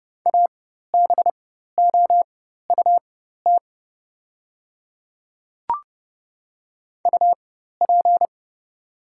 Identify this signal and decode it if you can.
Morse code, heard as a keyed tone, about 30 wpm, spaced as for 14 wpm, 700 Hz. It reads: ABOUT UP
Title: 4x Character Spacing - Top 100 Words - 30wpm